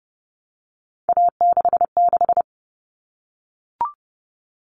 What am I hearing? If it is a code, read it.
Morse code, heard as a keyed tone, about 30 wpm, 700 Hz. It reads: A66
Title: All Cut Numbers - 3-Digit 30wpm